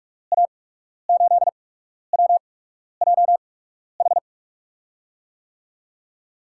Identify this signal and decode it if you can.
Morse code, heard as a keyed tone, about 47 wpm, spaced as for 12 wpm, 700 Hz. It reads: A8WJH
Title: USA Call Signs - 6 Characters - 8x Character Spacing - 45wpm